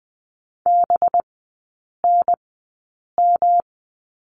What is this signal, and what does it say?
Morse code, heard as a keyed tone, about 19 wpm, spaced as for 8 wpm, 700 Hz. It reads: BNM
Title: 184 Lesson 30e - Letter b - Three Characters 20wpm